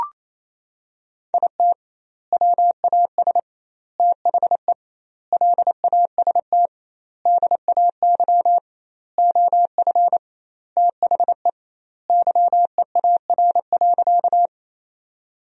Morse code, heard as a keyed tone, about 28 wpm, 700 Hz. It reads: IT WAS THE LAST DAY OF THE YEAR.
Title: 2x Word-Spacing - Sentences from Top 300 Words - 28wpm